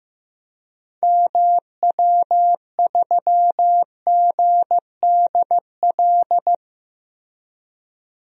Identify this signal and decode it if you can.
Morse code, heard as a keyed tone, about 15 wpm, 700 Hz. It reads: MW3GDL
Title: DXCC-100 Call Signs - 6 Characters - 15wpm